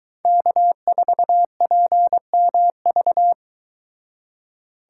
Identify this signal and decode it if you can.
Morse code, heard as a keyed tone, about 22 wpm, 700 Hz. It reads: K4PMV